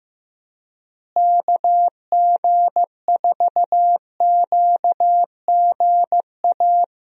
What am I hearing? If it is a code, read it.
Morse code, heard as a keyed tone, about 15 wpm, 700 Hz. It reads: KG4QGA
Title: USA Call Signs - 6 Characters - Repeated 3x - 15wpm